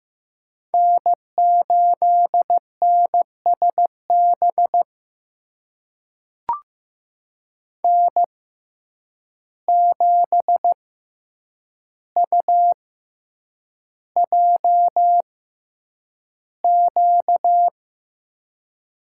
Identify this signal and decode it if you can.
Morse code, heard as a keyed tone, about 15 wpm, spaced as for 8 wpm, 700 Hz. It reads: N8NSB N7UJQ
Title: USA Call Signs - 5 Characters - 6x Character Spacing - 15wpm